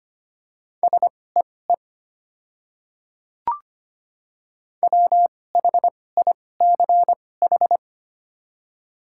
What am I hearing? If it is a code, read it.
Morse code, heard as a keyed tone, about 25 wpm, spaced as for 18 wpm, 690 Hz. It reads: SEE WHICH
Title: 2x Character Spacing - Top 100 Words - 25wpm